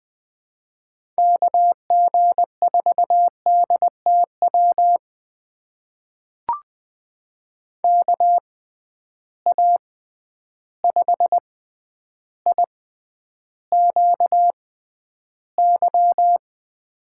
Morse code, heard as a keyed tone, about 20 wpm, spaced as for 10 wpm, 700 Hz. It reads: KG4DTW KA5IQY
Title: USA Call Signs - 6 Characters - 6x Character Spacing - 20wpm